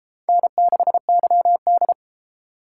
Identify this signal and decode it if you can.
Morse code, heard as a keyed tone, about 33 wpm, 700 Hz. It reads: N6YD